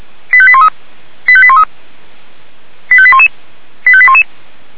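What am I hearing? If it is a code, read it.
Five-tone selective call (ZVEI), heard as quick runs of five tones, tones of about 70 ms. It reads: 85813 85813 85811 85811
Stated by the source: ZVEI five-tone selective call 85813 received on SDR.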